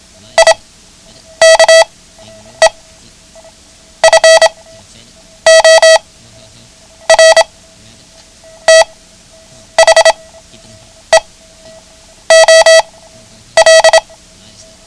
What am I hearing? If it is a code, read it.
Morse code, heard as a keyed tone, about 26 wpm, spaced as for 8 wpm, 650 Hz. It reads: IKE FOR THE OL
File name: ham radio / CodeQuick / Code Quick 09